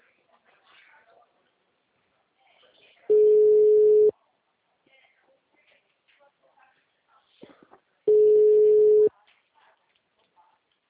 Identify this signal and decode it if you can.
Ringback tone (Europe, 425 Hz), with speech.